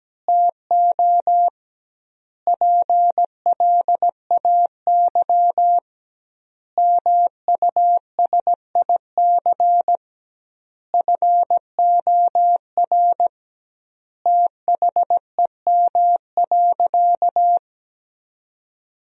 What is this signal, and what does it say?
Morse code, heard as a keyed tone, about 17 wpm, 700 Hz. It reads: TO PLAY MUSIC FOR THEM.
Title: 2x Word-Spacing - Sentences from Top 300 Words - 17wpm